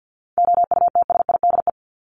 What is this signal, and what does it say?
Morse code, heard as a keyed tone, about 50 wpm, 700 Hz. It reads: OUTSIDE
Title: Top 751-1000 Words 50wpm